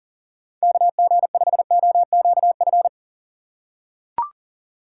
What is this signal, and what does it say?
Morse code, heard as a keyed tone, about 40 wpm, 700 Hz. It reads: KG5OQF
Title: USA Call Signs - 6 Characters - 2x Character Spacing - 40wpm